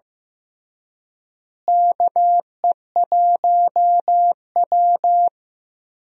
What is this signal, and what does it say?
Morse code, heard as a keyed tone, about 15 wpm, 700 Hz. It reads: KE1W
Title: USA Call Signs - 4 Characters - Repeated 3x - 15wpm